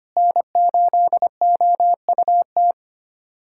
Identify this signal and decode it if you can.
Morse code, heard as a keyed tone, about 25 wpm, 700 Hz. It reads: N8OUT
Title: USA Callsign DE Callsign - 5 Characters - 25wpm